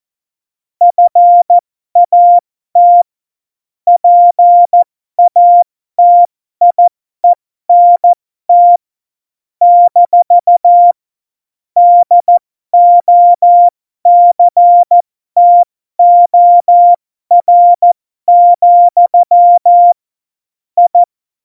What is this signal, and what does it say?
Morse code, heard as a keyed tone, about 14 wpm, spaced as for 12 wpm, 700 Hz. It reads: FAT PATIENT - DOCTOR, I
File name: Morsekurs / Witze